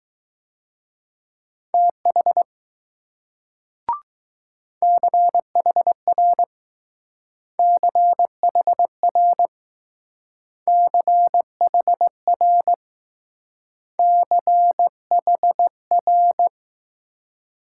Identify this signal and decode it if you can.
Morse code, heard as a keyed tone, about 20 wpm, 700 Hz. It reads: TH CHR CHR CHR CHR